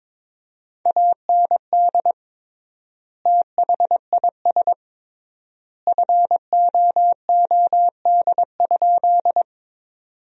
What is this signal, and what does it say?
Morse code, heard as a keyed tone, about 21 wpm, 700 Hz. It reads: AND THIS FOOD?
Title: 3x Word-Spacing - Sentences from Top 300 Words - 22wpm